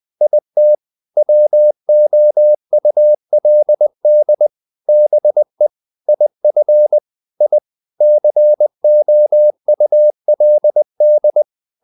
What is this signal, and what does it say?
Morse code, heard as a keyed tone, about 20 wpm, 600 Hz. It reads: IT WOULD BE IF I COULD
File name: the old man and the sea part 2 20wpm